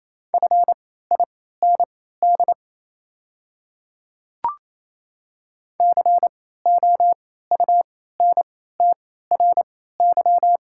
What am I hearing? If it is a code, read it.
Morse code, heard as a keyed tone, about 28 wpm, spaced as for 16 wpm, 700 Hz. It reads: FIND COUNTRY
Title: 3x Character Spacing - Top 300 Words - 28wpm